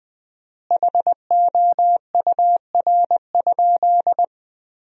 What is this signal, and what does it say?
Morse code, heard as a keyed tone, about 20 wpm, 700 Hz. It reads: HOUR?
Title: 4x Word-Spacing - Sentences from Top 400 Words - 20wpm